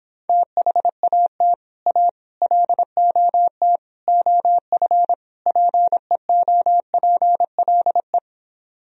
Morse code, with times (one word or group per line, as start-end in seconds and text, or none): 0.3-1.5 THAT
1.9-2.1 A
2.4-3.8 LOT
4.1-5.1 OF
5.5-8.2 PEOPLE